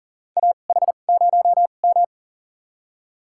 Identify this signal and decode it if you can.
Morse code, heard as a keyed tone, about 37 wpm, spaced as for 28 wpm, 700 Hz. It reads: AH0M
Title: USA Call Signs - 4 Characters - 2x Character Spacing - 40wpm